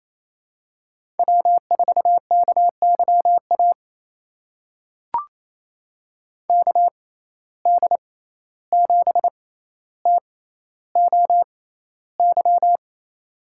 Morse code, heard as a keyed tone, about 28 wpm, spaced as for 14 wpm, 700 Hz. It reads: W4KYA KD7TOY
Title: USA Call Signs - 6x Character Spacing - 28wpm